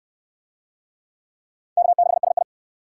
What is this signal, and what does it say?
Morse code, heard as a keyed tone, about 66 wpm, 700 Hz. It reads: K6SI